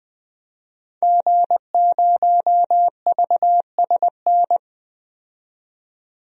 Morse code, heard as a keyed tone, about 20 wpm, 700 Hz. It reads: G0VSN
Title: DXCC-100 Call Signs - 5 Characters - 20wpm